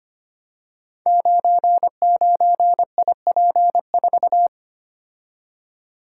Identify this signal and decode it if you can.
Morse code, heard as a keyed tone, about 25 wpm, 700 Hz. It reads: 99IP4
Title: Code Groups - 5 Letter:Numbers - 25wpm